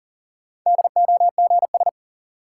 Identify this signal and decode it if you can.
Morse code, heard as a keyed tone, about 40 wpm, 700 Hz. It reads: DOGS
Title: Sentences from Top 1600 Words - New Word Drill Alt - 40wpm